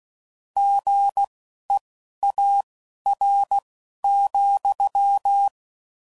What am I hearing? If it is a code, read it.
Morse code, heard as a keyed tone, about 16 wpm, spaced as for 11 wpm, 800 Hz. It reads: GEAR,